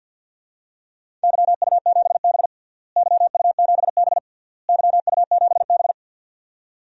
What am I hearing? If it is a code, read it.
Morse code, heard as a keyed tone, about 46 wpm, 700 Hz. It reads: YU7B YU7B YU7B
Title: DXCC-100 Call Signs - 4 Characters - Repeated 3x - 50wpm